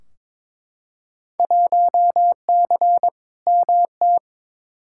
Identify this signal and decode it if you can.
Morse code, heard as a keyed tone, about 22 wpm, 700 Hz. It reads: 1C MT
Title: Field Day Exchanges Encore - 22wpm